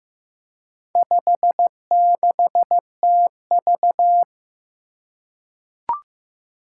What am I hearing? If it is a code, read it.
Morse code, heard as a keyed tone, about 15 wpm, 700 Hz. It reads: 56TV
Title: Four Challenging Letter-Number Combinations 15wpm